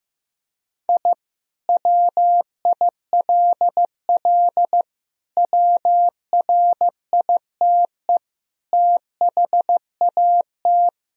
Morse code, heard as a keyed tone, about 15 wpm, 700 Hz. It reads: I WILL WRITE THAT
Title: Sentences from Top 1000 Words - Review Part 3 - 15wpm